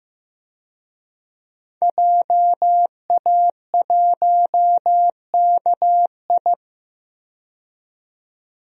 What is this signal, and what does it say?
Morse code, heard as a keyed tone, about 15 wpm, 700 Hz. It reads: JA1KI
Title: DXCC-100 Call Signs - 5 Characters - 15wpm